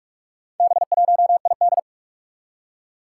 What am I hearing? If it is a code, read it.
Morse code, heard as a keyed tone, about 44 wpm, 700 Hz. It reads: B1ID
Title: Four Challenging Letter-Number Combinations 45wpm